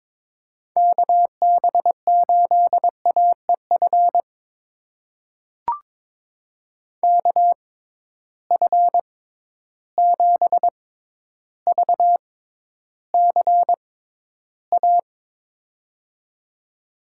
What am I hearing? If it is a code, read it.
Morse code, heard as a keyed tone, about 22 wpm, spaced as for 11 wpm, 700 Hz. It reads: KB8AEF KF7VCA
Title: USA Call Signs - 6 Characters - 6x Character Spacing - 22wpm